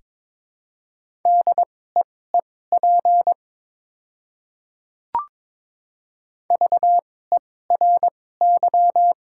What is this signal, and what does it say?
Morse code, heard as a keyed tone, about 21 wpm, spaced as for 16 wpm, 700 Hz. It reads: DEEP VERY
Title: 2x Character Spacing - Top 500 Words - 22wpm